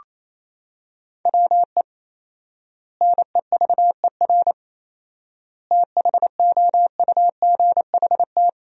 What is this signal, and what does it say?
Morse code, heard as a keyed tone, about 28 wpm, 700 Hz. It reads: WE NEVER THOUGHT
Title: 4x Word-Spacing - Sentences from Top 300 Words - 28wpm